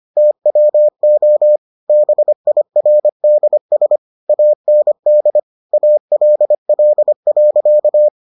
Morse code, heard as a keyed tone, about 25 wpm, 600 Hz. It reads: TWO BIRDS AND ALL.